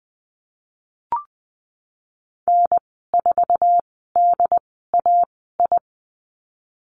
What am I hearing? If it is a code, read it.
Morse code, heard as a keyed tone, about 20 wpm, spaced as for 14 wpm, 700 Hz. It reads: N4DAI
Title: 076 Lesson 13g - Letter L - Call Signs 20wpm